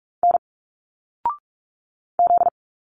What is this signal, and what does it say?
Morse code, heard as a keyed tone, about 46 wpm, 700 Hz. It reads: N Z
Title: Single Letters 45wpm